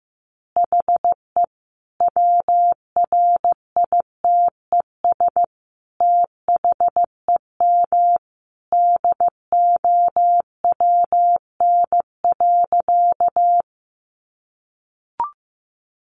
Morse code, heard as a keyed tone, about 15 wpm, 700 Hz. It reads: HE WRITES THEM DOWN.
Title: Additional Easy Phrases 15wpm